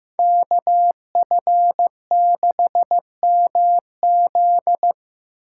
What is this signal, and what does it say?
Morse code, heard as a keyed tone, about 18 wpm, 700 Hz. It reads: KF6MZ
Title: Speed Racing - USA 2x2 Call Signs 15wpm